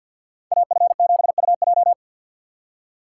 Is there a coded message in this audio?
Yes: Morse code from 0.5 to 1.9 s, heard as a keyed tone, about 64 wpm, 700 Hz. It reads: AF7UJ